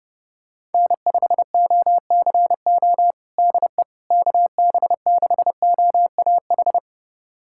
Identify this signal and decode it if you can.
Morse code, heard as a keyed tone, about 32 wpm, 700 Hz. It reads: N5OCO DE KB6OAH
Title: USA Callsign DE Callsign 30wpm